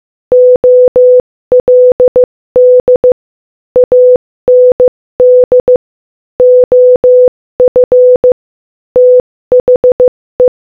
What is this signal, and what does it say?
Morse code, heard as a keyed tone, about 15 wpm, 500 Hz. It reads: OLD AND OF THE
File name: Morse-Code-Audio-Speed-15-3